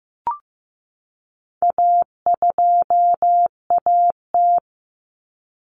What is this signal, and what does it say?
Morse code, heard as a keyed tone, about 15 wpm, 700 Hz. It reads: A2AT
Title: Common Cut Numbers - 4-Digit 15wpm